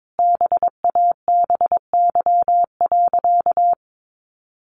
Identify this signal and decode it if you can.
Morse code, heard as a keyed tone, about 22 wpm, 700 Hz. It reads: BABY.
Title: Sentences from Top 1400 Words 22wpm